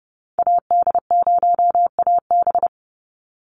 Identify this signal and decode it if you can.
Morse code, heard as a keyed tone, about 30 wpm, 700 Hz. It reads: AD0AB
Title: Call Signs - 30wpm